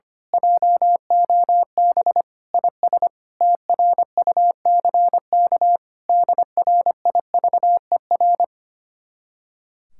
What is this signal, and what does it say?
Morse code, heard as a keyed tone, about 24 wpm, 700 Hz. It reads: JOB IS TRUCK DRIVER
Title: Speed-Racing Common QSO Phrases - 25wpm